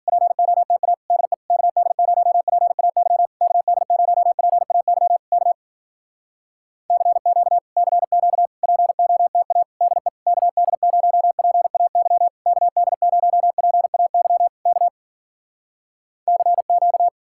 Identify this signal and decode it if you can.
Morse code, heard as a keyed tone, about 52 wpm, 700 Hz. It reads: POTA DE KD0PAY KD0PAY K CQ CQ POTA DE KD0PAY KD0PAY K CQ